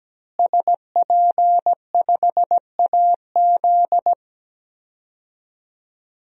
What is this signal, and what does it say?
Morse code, heard as a keyed tone, about 17 wpm, 700 Hz. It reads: SP5AZ
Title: DXCC-100 Call Signs - 5 Characters - Repeated 3x - 17wpm